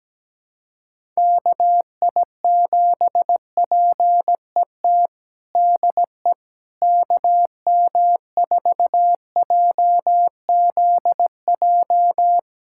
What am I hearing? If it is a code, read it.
Morse code, heard as a keyed tone, about 17 wpm, 700 Hz. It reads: KI7PET DE KM4JZJ
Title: USA Callsign DE Callsign - 6 Characters - 17wpm